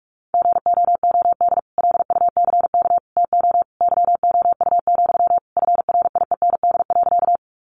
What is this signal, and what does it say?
Morse code, heard as a keyed tone, about 44 wpm, 700 Hz. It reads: GOOD LUCK TO YOU, FRIEND.
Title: Sentences from Top 1500 Words 45wpm